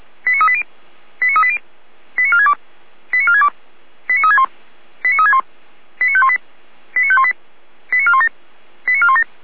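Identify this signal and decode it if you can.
Five-tone selective call (ZVEI), heard as quick runs of five tones, tones of about 70 ms. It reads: 89380 89380 89462 89462 89361 89361 89518 89518 89417 89417
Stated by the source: Five-tone ZVEI selective call received on an SDR.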